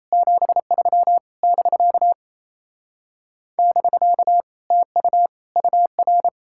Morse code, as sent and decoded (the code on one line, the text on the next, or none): --... ...-- / -...-.- / -...-.- / - ..- / ..- .-.
73 <BK> <BK> TU UR